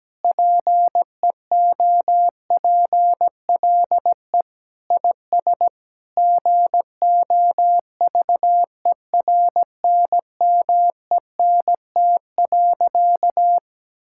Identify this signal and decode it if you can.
Morse code, heard as a keyed tone, about 17 wpm, 700 Hz. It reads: PEOPLE IS GOVERNMENT.
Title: Sentences from Top 1000 Words - Review Part 6 - 17wpm